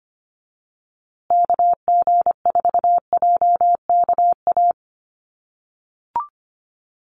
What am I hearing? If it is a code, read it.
Morse code, heard as a keyed tone, about 25 wpm, 700 Hz. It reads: KG4JKA
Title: USA Call Signs Repeated 2x 25wpm